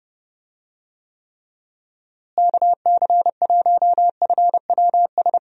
Morse code, heard as a keyed tone, about 30 wpm, 700 Hz. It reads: KC1FWS
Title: USA Call Signs - 6 Characters - Repeated 2x - 30wpm